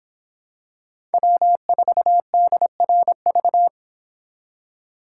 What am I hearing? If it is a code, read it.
Morse code, heard as a keyed tone, about 25 wpm, 700 Hz. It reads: W4DRV